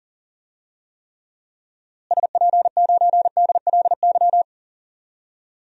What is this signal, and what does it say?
Morse code, heard as a keyed tone, about 40 wpm, 700 Hz. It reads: SP9DLY
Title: DXCC-100 Call Signs - 6 Characters - Repeated 3x - 40wpm